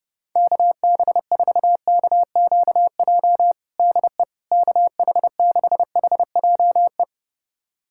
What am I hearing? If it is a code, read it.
Morse code, heard as a keyed tone, about 32 wpm, 700 Hz. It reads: KB4KQJ DE KH6HJE
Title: USA Callsign DE Callsign 30wpm